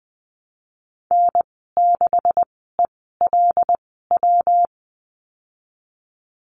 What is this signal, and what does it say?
Morse code, heard as a keyed tone, about 20 wpm, spaced as for 15 wpm, 700 Hz. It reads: N6ELW
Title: 179 Lesson 29g - Slash - Call Signs 20wpm